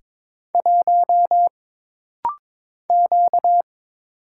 Morse code, heard as a keyed tone, about 22 wpm, 700 Hz. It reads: 1 Q
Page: This rapid-fire timing is geared to push your ICR (Instant Character Recognition) skills.